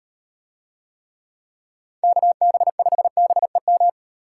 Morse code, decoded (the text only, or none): KB5BEM